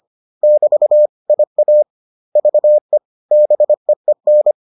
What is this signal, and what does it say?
Morse code, heard as a keyed tone, about 25 wpm, spaced as for 21 wpm, 600 Hz. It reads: <BT> IAVE BEEN